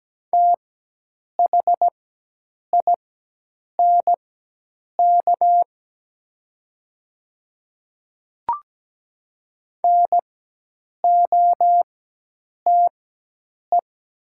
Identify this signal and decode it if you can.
Morse code, heard as a keyed tone, about 17 wpm, spaced as for 8 wpm, 700 Hz. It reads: THINK NOTE
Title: 4x Character Spacing - Top 500 Words - 17wpm